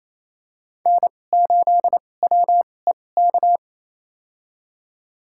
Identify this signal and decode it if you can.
Morse code, heard as a keyed tone, about 28 wpm, spaced as for 20 wpm, 700 Hz. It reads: N8WEK
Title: USA Call Signs - 5 Characters - 2x Character Spacing - 28wpm